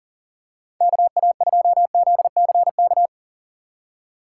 Morse code, heard as a keyed tone, about 40 wpm, 700 Hz. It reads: KA2ZCK
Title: USA Call Signs - 6 Characters - 3x Character Spacing - 40wpm